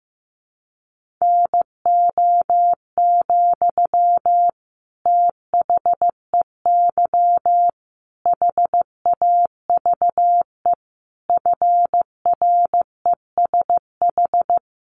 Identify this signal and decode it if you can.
Morse code, heard as a keyed tone, about 15 wpm, 700 Hz. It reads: NO, THEY HAVE FRESH
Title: Sentences from Top 1300 Words 15wpm